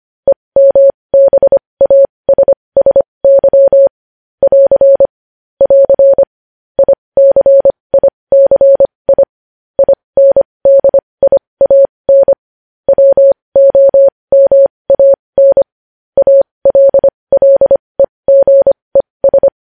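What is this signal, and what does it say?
Morse code, heard as a keyed tone, about 25 wpm, spaced as for 20 wpm, 550 Hz. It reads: EMBASSY <AR> <AR> ICICI INDIAN WOMAN ALLEGES
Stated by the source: Morse code at 20 wpm.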